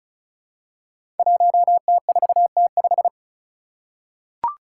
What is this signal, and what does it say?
Morse code, heard as a keyed tone, about 36 wpm, 700 Hz. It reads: 1T4T5
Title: Five Challenging Letter-Number Combinations 35wpm